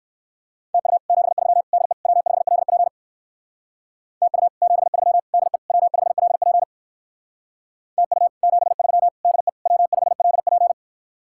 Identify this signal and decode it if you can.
Morse code, heard as a keyed tone, about 60 wpm, 700 Hz. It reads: TU 73 DE W5LP TU 73 DE W5LP TU 73 DE W5LP